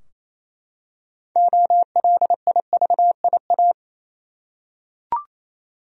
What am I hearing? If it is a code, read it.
Morse code, heard as a keyed tone, about 27 wpm, 700 Hz. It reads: OLIVIA
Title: USA Common Names - 28wpm